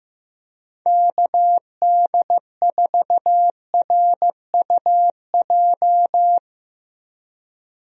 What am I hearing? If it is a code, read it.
Morse code, heard as a keyed tone, about 15 wpm, 700 Hz. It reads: KD4RUJ
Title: USA Call Signs - 6 Characters - Repeated 2x - 15wpm